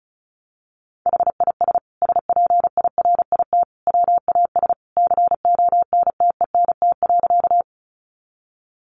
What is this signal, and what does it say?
Morse code, heard as a keyed tone, about 34 wpm, 700 Hz. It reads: HIS SPIRIT WAS CONTENT.
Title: Sentences from Top 1400 Words 35wpm